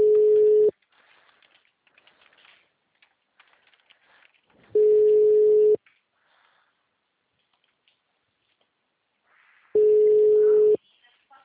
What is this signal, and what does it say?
Ringback tone (Europe, 425 Hz), with speech.
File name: out- -2007- - - .63